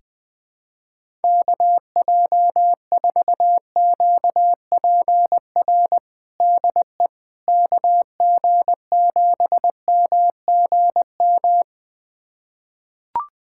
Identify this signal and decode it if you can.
Morse code, heard as a keyed tone, about 21 wpm, 700 Hz. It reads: KJ4QPR DE KG7MGM
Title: USA Callsign DE Callsign 20wpm